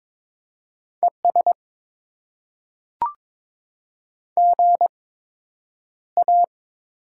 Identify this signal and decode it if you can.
Morse code, heard as a keyed tone, about 22 wpm, spaced as for 9 wpm, 700 Hz. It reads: ES GA
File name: Top 20 QSO Elements - 8x Character Spacing - 22wpm